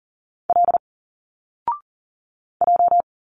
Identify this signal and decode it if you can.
Morse code, heard as a keyed tone, about 40 wpm, 700 Hz. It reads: L J
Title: Single Letters 40wpm